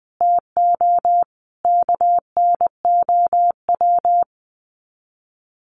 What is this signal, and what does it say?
Morse code, heard as a keyed tone, about 20 wpm, 700 Hz. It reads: TO KNOW
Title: Sets of 5 Words Encore 20wpm